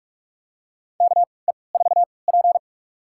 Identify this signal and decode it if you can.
Morse code, heard as a keyed tone, about 46 wpm, spaced as for 25 wpm, 700 Hz. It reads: KE4P
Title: USA Call Signs - 3x Character Spacing - 45wpm